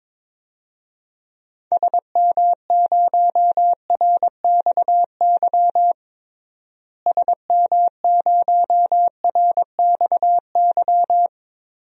Morse code, heard as a keyed tone, about 21 wpm, 700 Hz. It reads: SM0RXY SM0RXY
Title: DXCC-100 Call Signs - 6 Characters - Repeated 3x - 22wpm